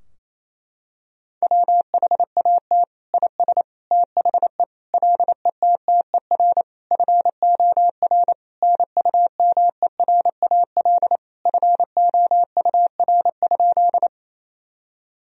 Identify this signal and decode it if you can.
Morse code, heard as a keyed tone, about 28 wpm, 700 Hz. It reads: WHAT IS THE LETTER FOR NUMERAL FOUR?
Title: Sentences from Top 1000 Words - Review Part 2 - 28wpm